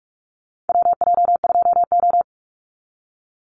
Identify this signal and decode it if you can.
Morse code, heard as a keyed tone, about 44 wpm, 700 Hz. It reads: WJ2O
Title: Call Signs - 45wpm